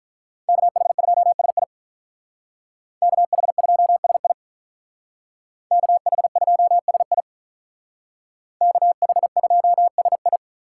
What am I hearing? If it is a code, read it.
Morse code, heard as a keyed tone, about 44 wpm, 700 Hz. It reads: KH2SI KH2SI KH2SI KH2SI
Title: Speed Racing - USA 2x2 Call Signs 35wpm